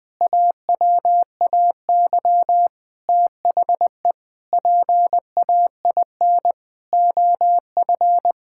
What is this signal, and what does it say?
Morse code, heard as a keyed tone, about 20 wpm, 700 Hz. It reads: AWAY THE PAIN OF
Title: Sentences from Top 1600 Words 20wpm